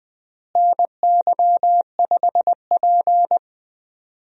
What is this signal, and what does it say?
Morse code, heard as a keyed tone, about 20 wpm, 700 Hz. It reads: NY5P
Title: USA Call Signs - 4 Characters - Repeated 2x - 20wpm